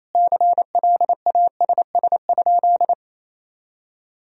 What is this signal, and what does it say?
Morse code, heard as a keyed tone, about 27 wpm, 700 Hz. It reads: CLASS?
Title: Sentences from Top 1000 Words - Review Part 2 - 28wpm